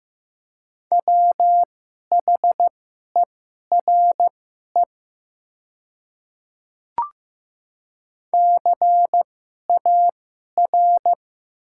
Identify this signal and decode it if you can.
Morse code, heard as a keyed tone, about 15 wpm, spaced as for 11 wpm, 700 Hz. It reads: WHERE CAR
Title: 2x Character Spacing - Top 500 Words - 15wpm